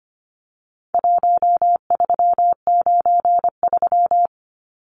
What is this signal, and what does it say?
Morse code, heard as a keyed tone, about 25 wpm, 700 Hz. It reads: 1393
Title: Numbers - 4-Digit 25wpm